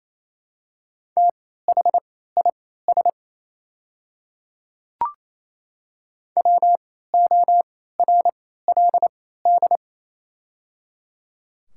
Morse code, heard as a keyed tone, about 28 wpm, spaced as for 16 wpm, 700 Hz. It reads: THIS WORLD
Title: 3x Character Spacing - Top 300 Words - 28wpm